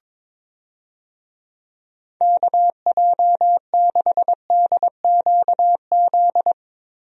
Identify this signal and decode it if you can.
Morse code, heard as a keyed tone, about 22 wpm, 700 Hz. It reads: KJ6DQZ
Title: USA Call Signs - 6 Characters - Repeated 2x - 22wpm